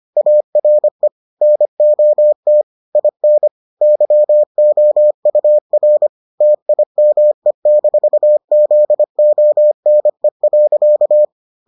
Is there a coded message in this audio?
Yes: Morse code from 0.2 to 11.2 s, heard as a keyed tone, about 25 wpm, 600 Hz. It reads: ARE NOT IN YOUR TIME-ZONE.